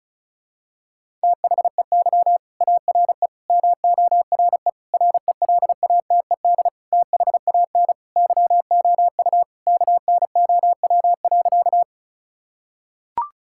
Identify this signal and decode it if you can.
Morse code, heard as a keyed tone, about 36 wpm, 700 Hz. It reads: THEY ARE MORE RELATED THAN YOU KNOW.
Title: Sentences from Top 1000 Words - Review Part 5 - 35wpm